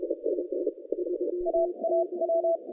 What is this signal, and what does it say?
Morse code, heard as a keyed tone, about 32 wpm, spaced as for 26 wpm, 660 Hz. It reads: AAW
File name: 2015 cq ww cw / 2015 cqwww k5tr - 0-4